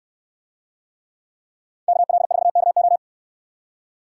Morse code, heard as a keyed tone, about 64 wpm, 700 Hz. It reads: KL4KO